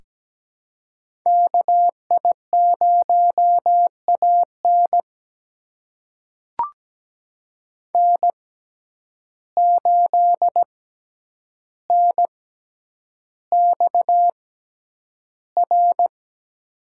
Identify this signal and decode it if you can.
Morse code, heard as a keyed tone, about 17 wpm, spaced as for 9 wpm, 700 Hz. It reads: KI0AN N8NXR
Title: USA Call Signs - 5 Characters - 6x Character Spacing - 17wpm